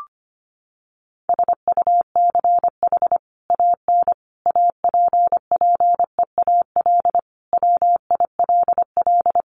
Morse code, heard as a keyed tone, about 25 wpm, 700 Hz. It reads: SUCH AN APPEAL WILL
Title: Sentences from Top 1300 Words 25wpm